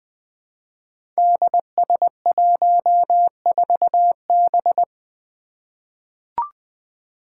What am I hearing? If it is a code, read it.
Morse code, heard as a keyed tone, about 20 wpm, 700 Hz. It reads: DS14B
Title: Five Challenging Letter-Number Combinations 20wpm